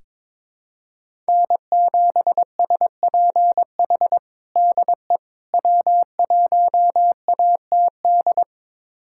Morse code, heard as a keyed tone, about 23 wpm, 700 Hz. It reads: N7SPH DE W1ATD
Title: USA Callsign DE Callsign - 5 Characters - 22wpm